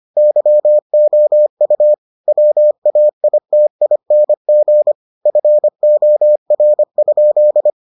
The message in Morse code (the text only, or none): YOU WAITING FOR?